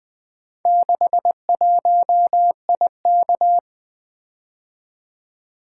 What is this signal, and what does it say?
Morse code, heard as a keyed tone, about 24 wpm, 700 Hz. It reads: 61IK
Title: Speed-Racing - Four Letter-Number Combinations - 20wpm